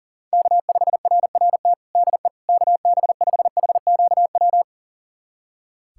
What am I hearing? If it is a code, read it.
Morse code, heard as a keyed tone, about 40 wpm, 700 Hz. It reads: K5RRT DE KB5HQW
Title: USA Callsign DE Callsign 40wpm